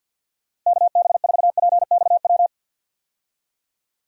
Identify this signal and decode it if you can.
Morse code, heard as a keyed tone, about 46 wpm, 700 Hz. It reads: KB4PXW
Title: USA Call Signs - 6 Characters - Repeated 2x - 50wpm